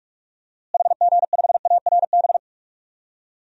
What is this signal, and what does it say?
Morse code, heard as a keyed tone, about 46 wpm, 700 Hz. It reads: HG5ARB
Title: DXCC-100 Call Signs - 6 Characters - Repeated 2x - 45wpm